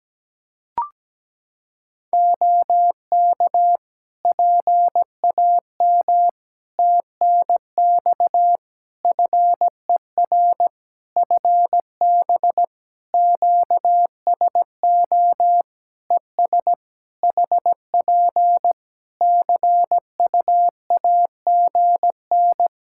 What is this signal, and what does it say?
Morse code, heard as a keyed tone, about 17 wpm, 700 Hz. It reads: OK PAM TNX FER FB QSO ES HP CUAGN
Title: K1USN SST QSO Practice 17wpm